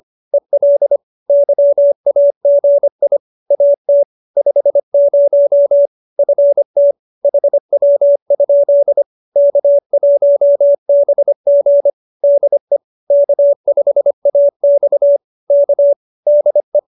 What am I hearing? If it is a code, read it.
Morse code, heard as a keyed tone, about 25 wpm, 570 Hz. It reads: EL YAGI AT 50 FT HW? K1BG DE K5AX K DE